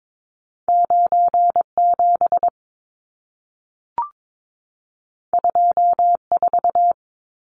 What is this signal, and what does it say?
Morse code, heard as a keyed tone, about 22 wpm, 700 Hz. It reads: 97 24
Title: Numbers - 2-Digit 22wpm